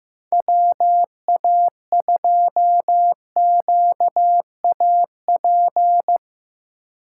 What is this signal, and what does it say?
Morse code, heard as a keyed tone, about 15 wpm, 700 Hz. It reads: WA2QAP
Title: USA Call Signs - 6 Characters - 2x Character Spacing - 15wpm